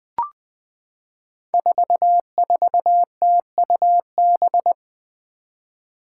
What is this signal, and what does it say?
Morse code, heard as a keyed tone, about 20 wpm, 700 Hz. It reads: 44TUB
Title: Five Challenging Letter-Number Combinations 20wpm